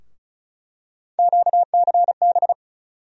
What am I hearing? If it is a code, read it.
Morse code, heard as a keyed tone, about 34 wpm, 700 Hz. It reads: QCB
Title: USA Call Sign Postfixes - 3 Letters 35wpm